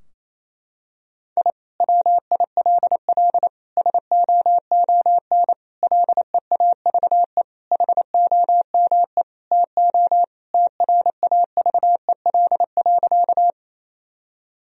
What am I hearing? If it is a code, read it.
Morse code, heard as a keyed tone, about 28 wpm, 700 Hz. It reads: I WILL SOON LEAVE HOME TO TRAVEL.
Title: Sentences from Top 1000 Words - Review Part 2 - 28wpm